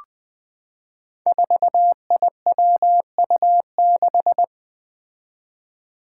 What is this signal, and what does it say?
Morse code, heard as a keyed tone, about 20 wpm, 700 Hz. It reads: 4IWU6
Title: Five Challenging Letter-Number Combinations 20wpm